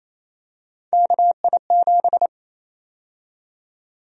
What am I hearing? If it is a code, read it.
Morse code, heard as a keyed tone, about 28 wpm, 700 Hz. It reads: KI7
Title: USA Call Sign Prefixes - 28wpm